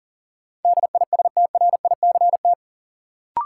Morse code, heard as a keyed tone, about 40 wpm, 700 Hz. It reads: DISTRICT
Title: Sentences from Top 1600 Words - New Word Drill Alt - 40wpm